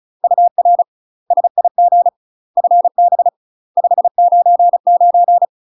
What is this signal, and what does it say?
Morse code, heard as a keyed tone, about 36 wpm, 700 Hz. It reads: UR SIG FB 599
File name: UK MASTER QSOs TRAINING / 35WPM / UK MASTER QSOs 0017